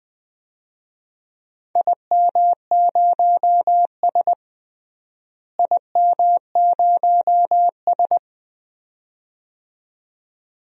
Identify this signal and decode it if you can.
Morse code, heard as a keyed tone, about 20 wpm, 700 Hz. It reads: IM0S IM0S
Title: DXCC-100 Call Signs - 4 Characters - Repeated 2x - 20wpm